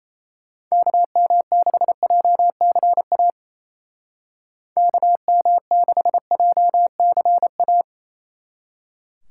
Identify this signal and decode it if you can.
Morse code, heard as a keyed tone, about 34 wpm, 700 Hz. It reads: KM6JCA KM6JCA
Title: Speed Racing - USA Call Signs 28wpm